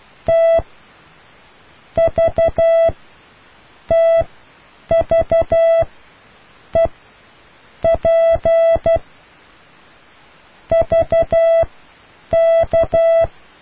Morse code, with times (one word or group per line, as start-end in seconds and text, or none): 0.3-0.6 T
2.0-2.9 V
3.9-5.8 TV
6.8-6.9 E
7.8-9.0 P
10.7-13.2 VK